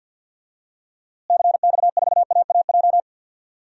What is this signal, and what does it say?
Morse code, heard as a keyed tone, about 50 wpm, 700 Hz. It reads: CX4AAJ
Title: DXCC-100 Call Signs - 6 Characters - Repeated 3x - 50wpm